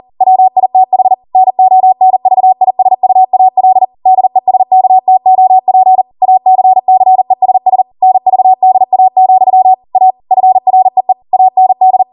Morse code, heard as a keyed tone, about 40 wpm, 750 Hz. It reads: WITH NONVISUAL DESKTOP ACCESS NVDA, A FREE AND